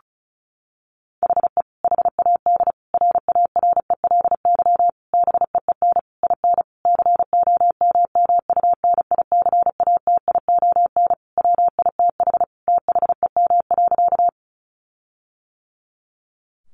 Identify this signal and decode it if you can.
Morse code, heard as a keyed tone, about 34 wpm, 700 Hz. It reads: HE HAD RARELY BEEN IN COMMUNICATION WITH THEM.
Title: Sentences from Top 1400 Words 35wpm